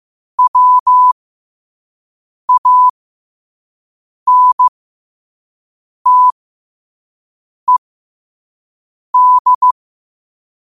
Morse code, heard as a keyed tone, about 15 wpm, spaced as for 5 wpm, 1000 Hz. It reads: WANTED
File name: qotdfiles / 5wpm